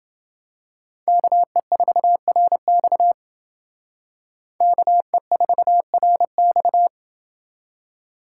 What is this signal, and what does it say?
Morse code, heard as a keyed tone, about 28 wpm, 700 Hz. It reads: KE4RX KE4RX